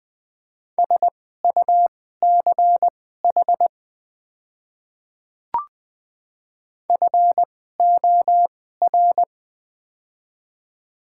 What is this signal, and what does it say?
Morse code, heard as a keyed tone, about 20 wpm, spaced as for 14 wpm, 700 Hz. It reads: SUCH FOR
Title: 2x Character Spacing - Top 200 Words - 20wpm